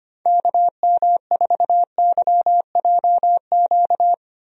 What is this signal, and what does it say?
Morse code, heard as a keyed tone, about 25 wpm, 700 Hz. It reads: KM4YJQ
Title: USA Call Signs - 6 Characters - Repeated 3x - 25wpm